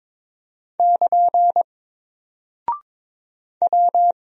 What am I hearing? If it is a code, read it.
Morse code, heard as a keyed tone, about 22 wpm, 700 Hz. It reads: <KN> W
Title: Single Character (Letter:Number:Punctuation:Prosign) 22wpm